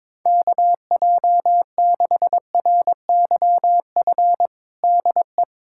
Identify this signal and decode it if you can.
Morse code, heard as a keyed tone, about 22 wpm, 700 Hz. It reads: KJ6RYF DE
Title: USA Callsign DE Callsign - 6 Characters - 22wpm